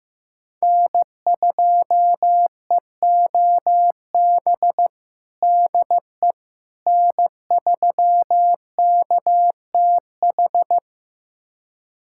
Morse code, heard as a keyed tone, about 15 wpm, 700 Hz. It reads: N2EOB DE N3KTH
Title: USA Callsign DE Callsign - 5 Characters - 15wpm